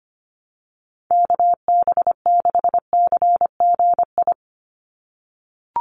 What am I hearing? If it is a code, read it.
Morse code, heard as a keyed tone, about 25 wpm, 700 Hz. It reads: KB6CGI
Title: USA Call Signs 25wpm